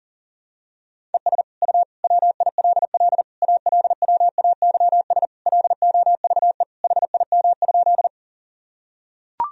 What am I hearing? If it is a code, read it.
Morse code, heard as a keyed tone, about 37 wpm, 700 Hz. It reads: ES U WILL ALWAYS LOVE HIM?
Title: Sentences from Top 400 Words and Abbreviations 40wpm